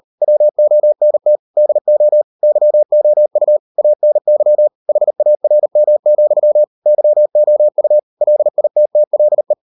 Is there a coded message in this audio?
Yes: Morse code from 0.2 to 9.5 s, heard as a keyed tone, about 38 wpm, 600 Hz. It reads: WONT DO YOU ANY HARM, YOU LITTLE